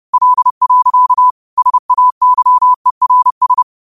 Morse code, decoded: L. SAYERS

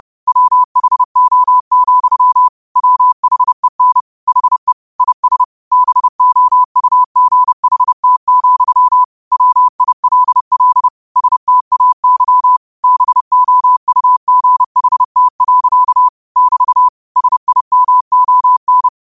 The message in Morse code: WHO, WHEN HE IS BOUGHT, WILL STAY BOUGHT. <BT> SIMON

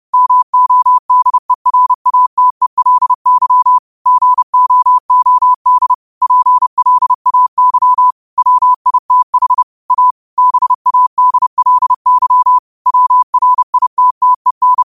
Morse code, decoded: MODERATELY GOOD PLAY WITH A BADLY WRITTEN

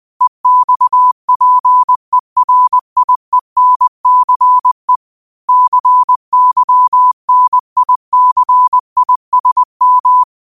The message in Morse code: EXPERIENCE CYNICISM

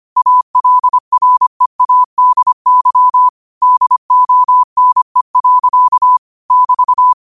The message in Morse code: ALREADY DONE. <BT>

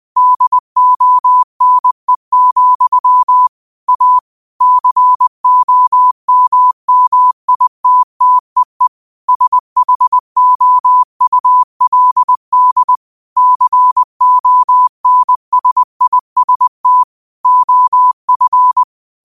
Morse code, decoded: DONE, A COMMITTEE SHOULD CONSIST OF

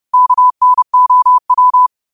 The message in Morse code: KNOW

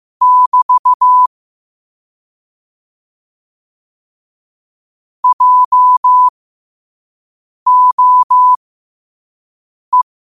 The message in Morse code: <BT> JOE

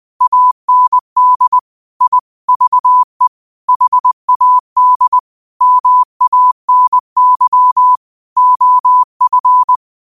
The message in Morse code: AND I VE HAD MANY OF

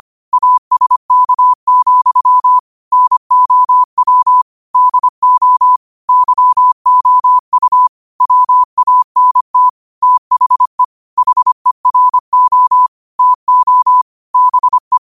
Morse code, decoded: ASK, NOW DO YOU WANT THE HERO TO BE

